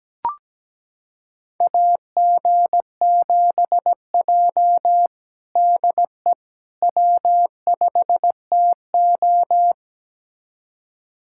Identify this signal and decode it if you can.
Morse code, heard as a keyed tone, about 17 wpm, 700 Hz. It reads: AG7J DE W5TO
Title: USA Callsign DE Callsign - 4 Characters - 17wpm